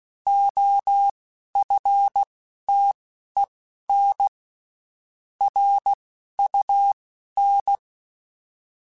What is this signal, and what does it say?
Morse code, heard as a keyed tone, about 16 wpm, spaced as for 11 wpm, 800 Hz. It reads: OFTEN RUN
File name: (spaced) / QSOs Spaced-mp3s / kc7haa @15 sp